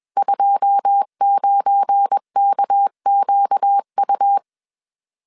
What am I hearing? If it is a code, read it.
Morse code, heard as a keyed tone, about 21 wpm, 800 Hz. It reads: 29KQU